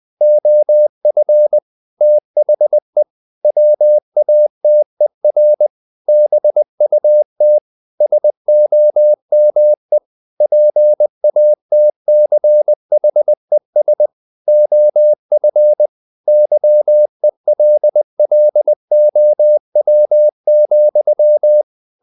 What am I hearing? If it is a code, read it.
Morse code, heard as a keyed tone, about 20 wpm, 600 Hz. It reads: OF THE WATER BUT SOME PATCHES OF YELLOW,
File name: the old man and the sea / the old man and the sea part 1 20wpm